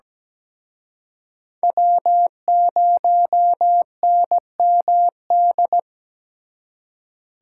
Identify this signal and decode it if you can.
Morse code, heard as a keyed tone, about 20 wpm, 700 Hz. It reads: W0NMD
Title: Speed Racing - USA 1x3 Call Signs 17wpm